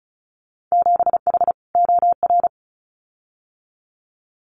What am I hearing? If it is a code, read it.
Morse code, heard as a keyed tone, about 34 wpm, 700 Hz. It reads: 7H OR
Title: Winter Field Day Exchanges Encore 35wpm